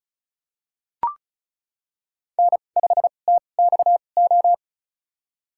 Morse code, heard as a keyed tone, about 34 wpm, spaced as for 25 wpm, 700 Hz. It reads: N5TXO
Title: USA Call Signs - 5 Characters - 2x Character Spacing - 35wpm